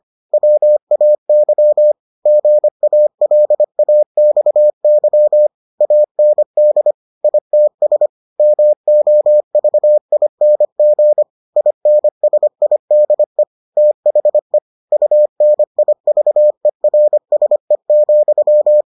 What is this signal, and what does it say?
Morse code, heard as a keyed tone, about 25 wpm, 600 Hz. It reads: WAY GALAXY AND ITS MOVING INSIDE THE UNIVERSE,